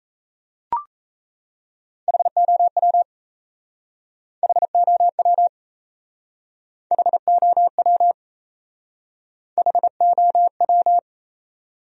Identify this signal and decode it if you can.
Morse code, heard as a keyed tone, about 35 wpm, 700 Hz. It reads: HOW HOW HOW HOW